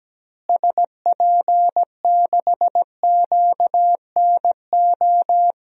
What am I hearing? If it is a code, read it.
Morse code, heard as a keyed tone, about 17 wpm, 700 Hz. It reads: SP6QNO
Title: DXCC-100 Call Signs - 6 Characters - Repeated 3x - 17wpm